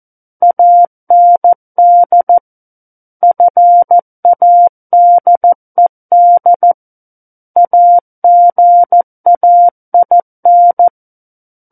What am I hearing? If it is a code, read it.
Morse code, heard as a keyed tone, about 14 wpm, 700 Hz. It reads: AND FADED AGAIN
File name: Book One / 05-Chapter 5 - 14 WPM